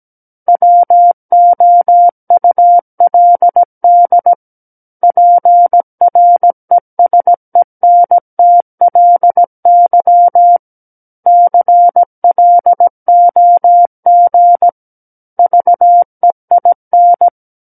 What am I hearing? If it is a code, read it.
Morse code, heard as a keyed tone, about 17 wpm, 700 Hz. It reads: WOULD PRESENTLY CLOG VEIN
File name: Book One / 08-Chapter 8 - 17 WPM